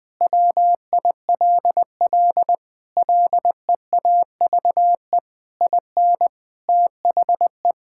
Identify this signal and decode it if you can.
Morse code, heard as a keyed tone, about 20 wpm, 700 Hz. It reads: WILL LEAVE IN THE
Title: Sentences from Top 1000 Words - Review Part 6 - 20wpm